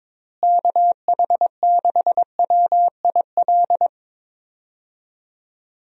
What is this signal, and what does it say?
Morse code, heard as a keyed tone, about 21 wpm, 700 Hz. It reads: KH6WIL
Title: USA Call Signs - 6 Characters - Repeated 3x - 22wpm